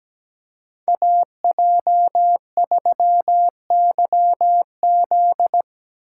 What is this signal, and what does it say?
Morse code, heard as a keyed tone, about 20 wpm, 700 Hz. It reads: AJ3YZ
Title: Speed Racing - USA 2x2 Call Signs 17wpm